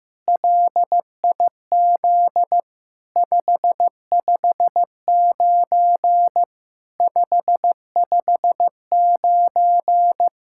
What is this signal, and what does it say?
Morse code, heard as a keyed tone, about 15 wpm, 700 Hz. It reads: LIZ 559 559
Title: SOTA QSO Practice 15wpm